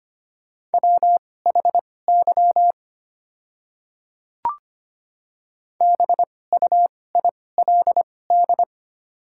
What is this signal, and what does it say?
Morse code, heard as a keyed tone, about 25 wpm, spaced as for 18 wpm, 700 Hz. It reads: WHY BUILD
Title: 2x Character Spacing - Top 200 Words - 25wpm